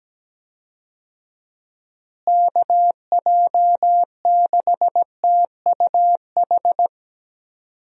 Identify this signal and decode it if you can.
Morse code, heard as a keyed tone, about 17 wpm, 700 Hz. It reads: KJ6TUH
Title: USA Call Signs - 6 Characters - Repeated 3x - 17wpm